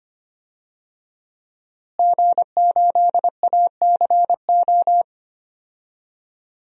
Morse code, heard as a keyed tone, about 25 wpm, 700 Hz. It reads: G8ACO
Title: DXCC-100 Call Signs - 5 Characters - Repeated 3x - 25wpm